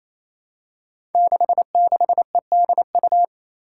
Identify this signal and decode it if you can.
Morse code, heard as a keyed tone, about 28 wpm, 700 Hz. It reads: 66EDU
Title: Five Challenging Letter-Number Combinations 28wpm